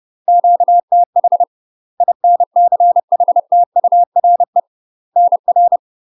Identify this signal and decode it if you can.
Morse code, heard as a keyed tone, about 30 wpm, 700 Hz. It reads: QTH INCHTURE NR
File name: UK MASTER QSOs TRAINING / 30WPM / UK MASTER QSOs 0047